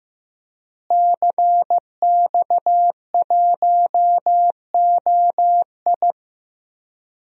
Morse code, heard as a keyed tone, about 15 wpm, 700 Hz. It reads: CX1OI
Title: DXCC-100 Call Signs - 5 Characters - Repeated 2x - 15wpm